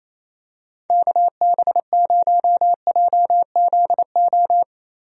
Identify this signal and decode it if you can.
Morse code, heard as a keyed tone, about 28 wpm, 700 Hz. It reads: KB0JZO
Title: USA Call Signs - 6 Characters - Repeated 2x - 28wpm